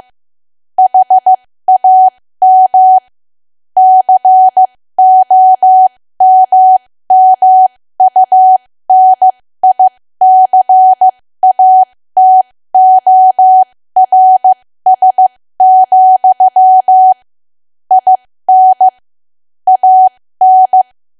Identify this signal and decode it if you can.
Morse code, heard as a keyed tone, about 15 wpm, spaced as for 13 wpm, 750 Hz. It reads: HAM COMMUNICATORS, IN AN
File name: cwa / W1AW-Level2 / 13WPM